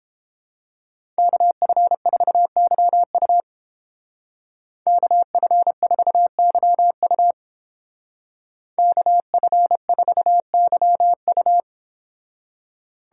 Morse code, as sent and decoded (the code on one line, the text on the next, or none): -.- ..-. ....- -.-- ..- / -.- ..-. ....- -.-- ..- / -.- ..-. ....- -.-- ..-
KF4YU KF4YU KF4YU